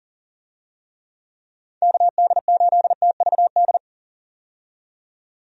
Morse code, heard as a keyed tone, about 40 wpm, 700 Hz. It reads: KD8TVD
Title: USA Call Signs - 6 Characters - 40wpm